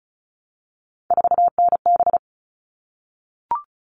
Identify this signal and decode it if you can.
Morse code, heard as a keyed tone, about 34 wpm, 700 Hz. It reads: 4NB
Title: All Cut Numbers - 3-Digit 35wpm